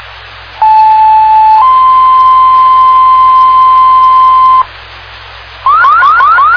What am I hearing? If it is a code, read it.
Two-tone page, heard as a one-second tone, then a three-second tone, about 870 then 1040 Hz. It is heard from 0.6 to 4.6 s.